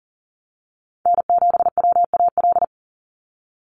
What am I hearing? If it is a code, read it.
Morse code, heard as a keyed tone, about 40 wpm, 690 Hz. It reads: N7WAL
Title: USA Call Signs 40wpm